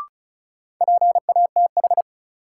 Morse code, heard as a keyed tone, about 36 wpm, 700 Hz. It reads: PATH
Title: Sentences from Top 1400 Words - New Word Drill - 35wpm